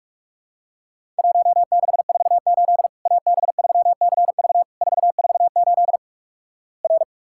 Morse code, heard as a keyed tone, about 44 wpm, 700 Hz. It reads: 1648 AB3CV 448 R
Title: Worked All Europe DX Contest - QTC Exchange - 45wpm